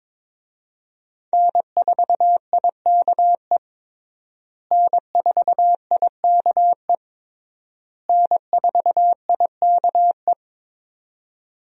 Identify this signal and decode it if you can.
Morse code, heard as a keyed tone, about 22 wpm, 700 Hz. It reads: N4IKE N4IKE N4IKE
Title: USA Call Signs - 5 Characters - Repeated 3x - 22wpm